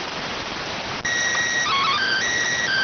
SSTV (slow-scan television), PD 290.